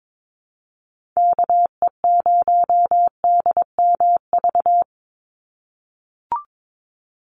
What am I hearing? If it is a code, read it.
Morse code, heard as a keyed tone, about 22 wpm, 700 Hz. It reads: KE0DMV
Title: USA Call Signs 22wpm